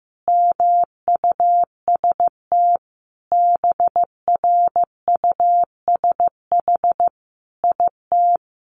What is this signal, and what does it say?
Morse code, heard as a keyed tone, about 15 wpm, 700 Hz. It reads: MUST BRUSH IT
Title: Sentences from Top 1100 Words 15wpm